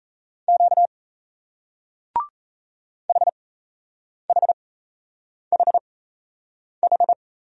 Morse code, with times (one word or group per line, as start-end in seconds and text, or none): 0.5-0.9 Q
3.1-7.1 HHHH